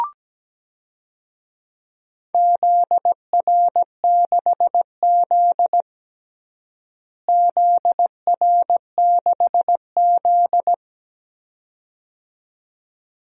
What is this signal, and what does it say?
Morse code, heard as a keyed tone, about 17 wpm, 700 Hz. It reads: ZR6Z ZR6Z
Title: DXCC-100 Call Signs - 4 Characters - Repeated 2x - 17wpm